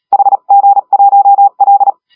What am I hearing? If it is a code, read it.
Morse code, heard as a keyed tone, about 35 wpm, 800 Hz. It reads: HG1L
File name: AUDIO2019 / HG1R CW SSB MIX